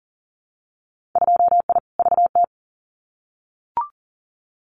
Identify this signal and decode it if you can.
Morse code, heard as a keyed tone, about 38 wpm, 700 Hz. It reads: 2I VT
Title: Winter Field Day Exchanges 40wpm